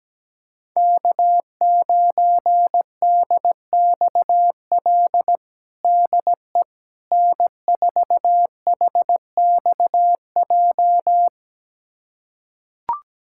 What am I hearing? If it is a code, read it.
Morse code, heard as a keyed tone, about 17 wpm, 700 Hz. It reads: K9DXL DE N4HXJ
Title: USA Callsign DE Callsign - 5 Characters - 17wpm